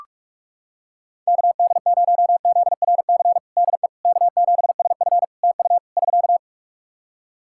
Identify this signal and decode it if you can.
Morse code, heard as a keyed tone, about 46 wpm, 700 Hz. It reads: KD0ZRC DE K7SF TU <SK>
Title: Practice QSO - Exchange 4 - Ending - 45wpm